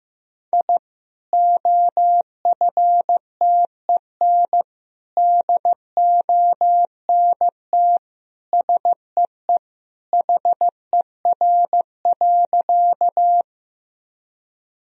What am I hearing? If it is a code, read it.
Morse code, heard as a keyed tone, about 15 wpm, 700 Hz. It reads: I OFTEN DONT SEE HER.
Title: Sentences from Top 1000 Words - Review Part 1 - 15wpm